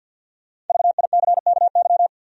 Morse code, heard as a keyed tone, about 47 wpm, 700 Hz. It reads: VICKY
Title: USA Common Names - Max 5 Letters - 50wpm